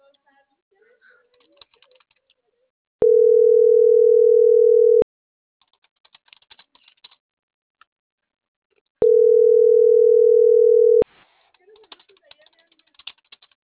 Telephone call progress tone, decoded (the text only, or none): ringback tone